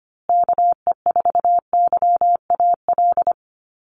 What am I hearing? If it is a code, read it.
Morse code, heard as a keyed tone, about 25 wpm, 700 Hz. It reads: KE4YAL
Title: USA Call Signs Repeated 3x 25wpm